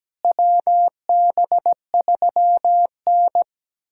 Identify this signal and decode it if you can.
Morse code, heard as a keyed tone, about 17 wpm, 700 Hz. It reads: WB3N
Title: USA Callsign DE Callsign - 4 Characters - 17wpm